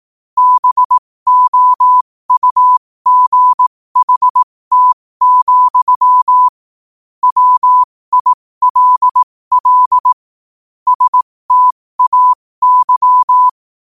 Morse code, heard as a keyed tone, about 18 wpm, spaced as for 15 wpm, 1000 Hz. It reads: BOUGHT, WILL STAY
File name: qotdfiles / 15wpm